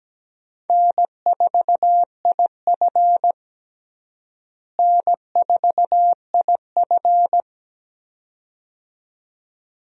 Morse code, heard as a keyed tone, about 17 wpm, 700 Hz. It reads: N4IF N4IF
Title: USA Call Signs - 4 Characters - Repeated 2x - 17wpm